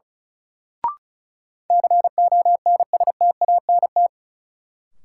Morse code, heard as a keyed tone, about 36 wpm, 700 Hz. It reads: CONSTANT
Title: Sentences from Top 1500 Words - New Word Drill - 35wpm